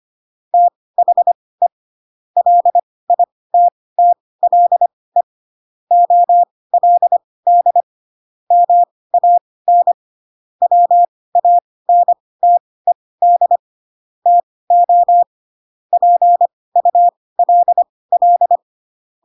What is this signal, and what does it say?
Morse code, heard as a keyed tone, about 25 wpm, spaced as for 18 wpm, 700 Hz. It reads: THE LITTLE OLD MAN WANTED TO PULL
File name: Pinocchio ch3 12 25 ik1hge